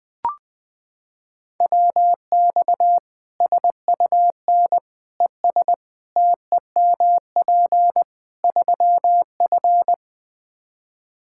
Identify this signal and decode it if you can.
Morse code, heard as a keyed tone, about 20 wpm, 700 Hz. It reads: WX SUN ES TEMP 3F
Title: K1USN SST QSO Practice 20wpm